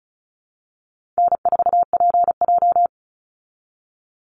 Morse code, heard as a keyed tone, about 34 wpm, 700 Hz. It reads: N4PJ
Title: Easy Call Signs Repeated 3x - 35wpm